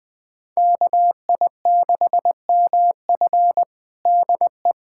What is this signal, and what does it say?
Morse code, heard as a keyed tone, about 20 wpm, 700 Hz. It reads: KI6MF DE
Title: USA Callsign DE Callsign - 5 Characters - 20wpm